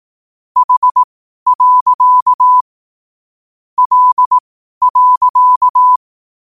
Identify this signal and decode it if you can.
Morse code, heard as a keyed tone, about 18 wpm, spaced as for 12 wpm, 1000 Hz. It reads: H. L.
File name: qotdfiles / 12wpm